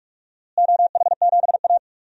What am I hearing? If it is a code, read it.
Morse code, heard as a keyed tone, about 46 wpm, 700 Hz. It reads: OH7A
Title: DXCC-100 Call Signs - 4 Characters - Repeated 3x - 45wpm